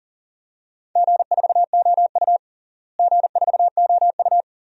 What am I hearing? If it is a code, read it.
Morse code, heard as a keyed tone, about 37 wpm, 700 Hz. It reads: G4OU G4OU
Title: DXCC-100 Call Signs - 4 Characters - Repeated 2x - 40wpm